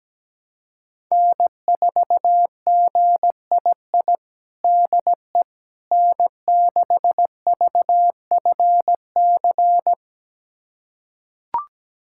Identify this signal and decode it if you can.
Morse code, heard as a keyed tone, about 17 wpm, 700 Hz. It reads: N4GII DE N6VFC
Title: USA Callsign DE Callsign - 5 Characters - 17wpm